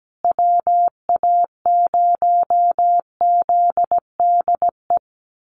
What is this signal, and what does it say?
Morse code, heard as a keyed tone, about 17 wpm, 700 Hz. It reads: WA0ZDE
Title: USA Call Signs Repeated 3x 17wpm